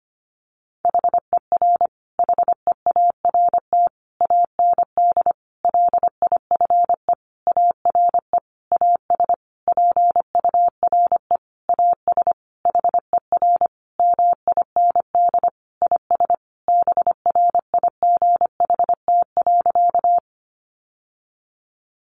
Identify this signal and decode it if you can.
Morse code, heard as a keyed tone, about 25 wpm, 700 Hz. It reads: HER HEART AND LIFE ARE AS PURE AS HER MIND IS BRIGHT.
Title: Sentences from Top 1500 Words 25wpm